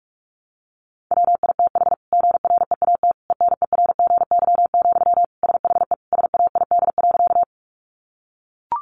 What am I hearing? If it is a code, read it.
Morse code, heard as a keyed tone, about 44 wpm, 700 Hz. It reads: WITH GREAT ENERGY, SHE SAID.
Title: Sentences from Top 900 Words 45wpm